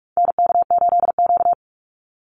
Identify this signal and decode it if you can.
Morse code, heard as a keyed tone, about 44 wpm, 700 Hz. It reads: NK8Q
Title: NA SOTA Activator Callsigns 45wpm